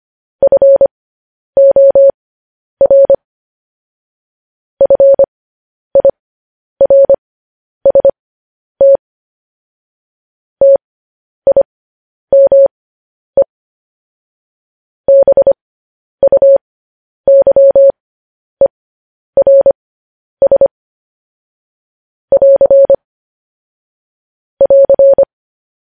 Morse code, heard as a keyed tone, about 25 wpm, spaced as for 10 wpm, 550 Hz. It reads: FOR FIRST TIME BUYERS <AR> <AR>
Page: news-headlines-in-morse-code-at-10-wpm